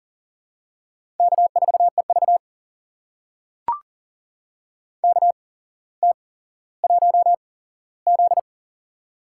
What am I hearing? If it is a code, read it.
Morse code, heard as a keyed tone, about 42 wpm, spaced as for 17 wpm, 700 Hz. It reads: K4EV KT1Z
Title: USA Call Signs - 4 Characters - 8x Character Spacing - 40wpm